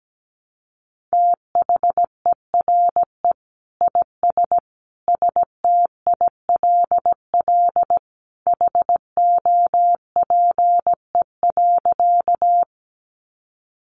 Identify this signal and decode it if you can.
Morse code, heard as a keyed tone, about 17 wpm, 700 Hz. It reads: THERE IS STILL HOPE.
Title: Sentences from Top 1000 Words 17wpm